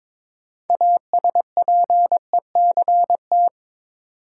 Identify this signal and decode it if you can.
Morse code, heard as a keyed tone, about 22 wpm, 700 Hz. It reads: ASPECT
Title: Sentences from Top 1300 Words - New Word Drill Alt - 22wpm